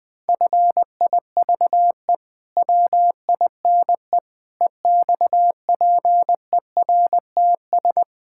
Morse code, heard as a keyed tone, about 20 wpm, 700 Hz. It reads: FIVE WINE EXPERTS
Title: Pangram Challenge v2 - 20wpm